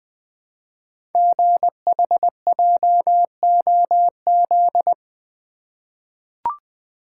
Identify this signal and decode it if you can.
Morse code, heard as a keyed tone, about 20 wpm, 700 Hz. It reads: GHJOZ
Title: Code Groups - 5 Letters - 20wpm